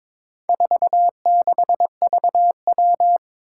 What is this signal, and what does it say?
Morse code, heard as a keyed tone, about 22 wpm, 700 Hz. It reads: 46VW
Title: Four Challenging Letter-Number Combinations 22wpm